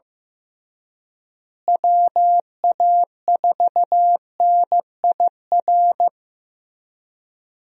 Morse code, heard as a keyed tone, about 15 wpm, 700 Hz. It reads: WA4NIR
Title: USA Call Signs - 6 Characters - Repeated 3x - 15wpm